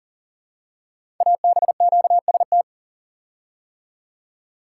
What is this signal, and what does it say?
Morse code, heard as a keyed tone, about 42 wpm, 700 Hz. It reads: ABQST
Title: Code Groups - 5 Letters - 40wpm